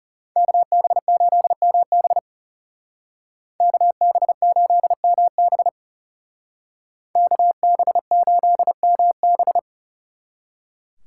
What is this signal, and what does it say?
Morse code, heard as a keyed tone, about 37 wpm, 700 Hz. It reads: KB8MB KB8MB KB8MB
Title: Speed Racing - USA 2x2 Call Signs 30wpm